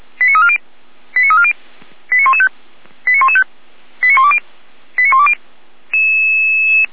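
Five-tone selective call (ZVEI), heard as quick runs of five tones, tones of about 70 ms. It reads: 89340 89340 89115 89115 89120 89120
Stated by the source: ZVEI five-tone call 89025 received by SDR, recorded by Feuerwehr Orlamünde server.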